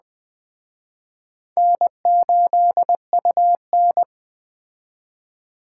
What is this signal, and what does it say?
Morse code, heard as a keyed tone, about 20 wpm, 700 Hz. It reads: N8UN
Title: Speed Racing - USA 1x2 Call Signs 17wpm